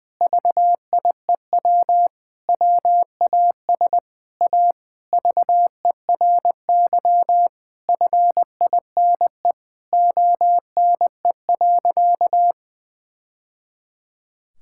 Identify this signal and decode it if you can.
Morse code, heard as a keyed tone, about 21 wpm, 700 Hz. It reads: VIEW WAS A VERY FINE ONE.
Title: Sentences from Top 1000 Words - Review Part 6 - 20wpm